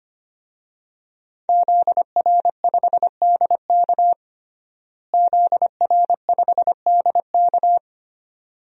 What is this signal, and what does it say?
Morse code, heard as a keyed tone, about 25 wpm, 700 Hz. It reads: ZR5DK ZR5DK
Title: DXCC-100 Call Signs - 5 Characters - Repeated 2x - 25wpm